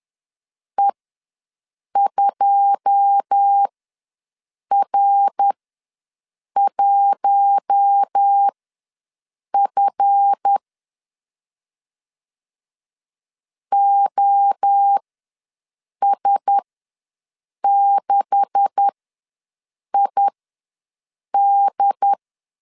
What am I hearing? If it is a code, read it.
Morse code, heard as a keyed tone, about 10 wpm, spaced as for 6 wpm, 800 Hz. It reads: E2R1F OS6ID
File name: LEZIONI / 9 286 ws 10wpm 4032